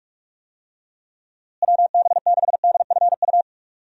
Morse code, heard as a keyed tone, about 46 wpm, 700 Hz. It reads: WB6DFU
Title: USA Call Signs - 6 Characters - Repeated 2x - 45wpm